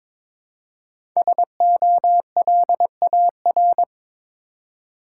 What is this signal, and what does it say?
Morse code, heard as a keyed tone, about 22 wpm, 700 Hz. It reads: SOLAR
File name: 5 Letter Words - First 3 Letters Given - 22wpm